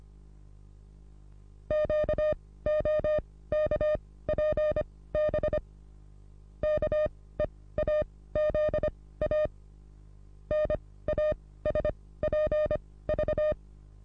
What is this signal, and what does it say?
Morse code, heard as a keyed tone, about 26 wpm, spaced as for 17 wpm, 610 Hz. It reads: QOKPB KEAZA NASPV
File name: repos / Morsekurs / DJ4UF Morsen27